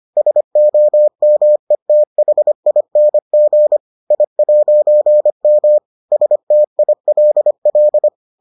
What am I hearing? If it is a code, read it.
Morse code, heard as a keyed tone, about 25 wpm, 600 Hz. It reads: SOMETHING I'M STILL